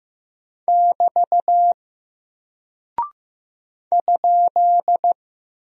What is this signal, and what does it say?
Morse code, heard as a keyed tone, about 15 wpm, 700 Hz. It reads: <BT> ?
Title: Single Character (Letter:Number:Punctuation:Prosign) Alt 15wpm